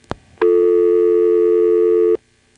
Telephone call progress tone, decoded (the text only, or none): dial tone